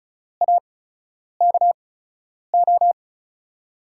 Morse code, heard as a keyed tone, about 36 wpm, spaced as for 10 wpm, 700 Hz. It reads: AKO
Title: USA Call Signs - 5 Characters - 8x Character Spacing - 35wpm